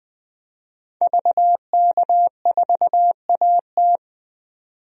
Morse code, heard as a keyed tone, about 21 wpm, 700 Hz. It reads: VK4AT
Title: DXCC-100 Call Signs - 5 Characters - 20wpm